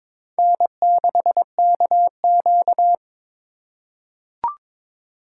Morse code, heard as a keyed tone, about 22 wpm, 700 Hz. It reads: N6KQ
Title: USA Call Signs - 4 Characters - 4x Character Spacing - 22wpm